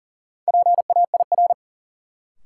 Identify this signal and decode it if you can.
Morse code, heard as a keyed tone, about 40 wpm, 700 Hz. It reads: PAIR
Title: Sets of 3 Words - New Word Drill - 40wpm